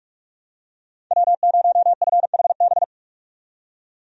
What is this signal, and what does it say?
Morse code, heard as a keyed tone, about 45 wpm, 700 Hz. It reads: W0FHB
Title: USA Call Signs - 5 Characters - 45wpm